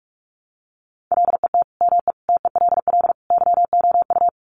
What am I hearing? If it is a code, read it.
Morse code, heard as a keyed tone, about 44 wpm, 700 Hz. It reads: LET ME TELL YOU
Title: Sets of 4 Words 45wpm